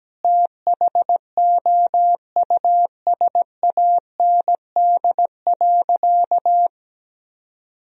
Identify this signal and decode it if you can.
Morse code, heard as a keyed tone, about 17 wpm, 700 Hz. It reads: THOUSAND.
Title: Sentences from Top 1000 Words - Review Part 3 - 17wpm